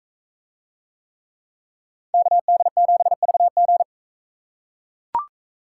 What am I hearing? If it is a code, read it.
Morse code, heard as a keyed tone, about 42 wpm, 700 Hz. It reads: KD7VG